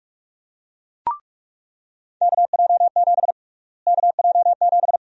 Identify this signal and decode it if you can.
Morse code, heard as a keyed tone, about 42 wpm, 700 Hz. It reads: KJ7 KJ7
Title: USA Call Sign Prefixes - Repeated 2x - 45wpm